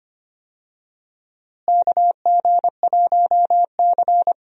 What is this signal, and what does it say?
Morse code, heard as a keyed tone, about 25 wpm, 700 Hz. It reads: KG1C
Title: USA Call Signs - 4 Characters - Repeated 3x - 25wpm